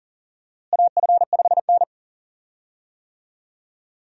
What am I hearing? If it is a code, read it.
Morse code, heard as a keyed tone, about 40 wpm, 700 Hz. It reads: AF5N
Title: USA Call Signs - 4 Characters - Repeated 2x - 40wpm